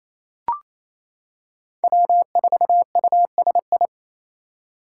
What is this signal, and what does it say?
Morse code, heard as a keyed tone, about 28 wpm, 700 Hz. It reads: W4USI
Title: K1USN SST QSO Practice - 28wpm